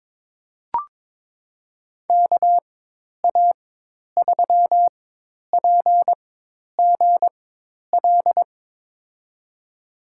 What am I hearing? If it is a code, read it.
Morse code, heard as a keyed tone, about 22 wpm, spaced as for 10 wpm, 700 Hz. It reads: KA3PGL
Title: USA Call Signs - 6 Characters - 4x Character Spacing - 22wpm